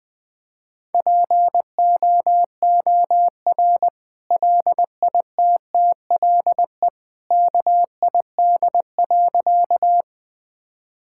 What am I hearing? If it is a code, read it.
Morse code, heard as a keyed tone, about 21 wpm, 700 Hz. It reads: POOR LITTLE KID.
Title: Sentences from Top 1600 Words 20wpm